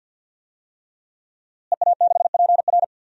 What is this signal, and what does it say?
Morse code, heard as a keyed tone, about 50 wpm, 700 Hz. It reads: EA6PR
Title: DXCC-100 Call Signs - 5 Characters - Repeated 3x - 50wpm